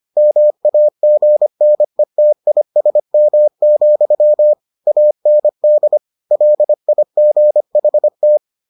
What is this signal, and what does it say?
Morse code, heard as a keyed tone, about 25 wpm, 600 Hz. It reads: MAGNETISM, AND LIGHT